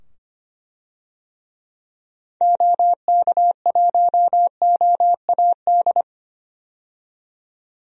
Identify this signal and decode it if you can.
Morse code, heard as a keyed tone, about 25 wpm, 700 Hz. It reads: OK1OAD
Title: DXCC-100 Call Signs - 6 Characters - 25wpm